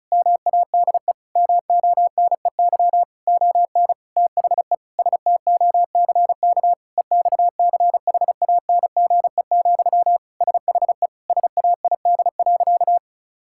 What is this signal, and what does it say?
Morse code, heard as a keyed tone, about 34 wpm, 690 Hz. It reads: MADE MONEY ON THE STOCK EXCHANGE, SHE SAID.
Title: Sentences from Top 1000 Words - Review Part 6 - 35wpm